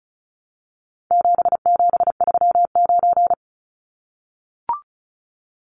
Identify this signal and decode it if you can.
Morse code, heard as a keyed tone, about 34 wpm, 700 Hz. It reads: 7739
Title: Numbers - 4-Digit 35wpm